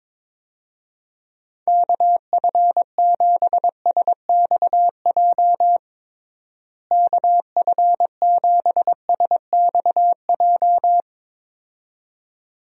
Morse code, heard as a keyed tone, about 22 wpm, 700 Hz. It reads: KF7SXJ KF7SXJ
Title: USA Call Signs - 6 Characters - Repeated 2x - 22wpm